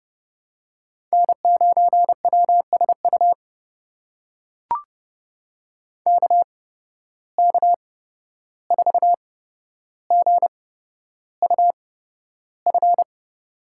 Morse code, heard as a keyed tone, about 30 wpm, spaced as for 12 wpm, 700 Hz. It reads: N9WSU KK4GUF
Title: USA Call Signs - 8x Character Spacing - 30wpm